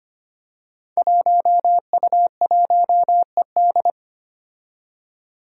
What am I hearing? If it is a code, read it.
Morse code, heard as a keyed tone, about 25 wpm, 700 Hz. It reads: 1U1ED
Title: Five Challenging Letter-Number Combinations 25wpm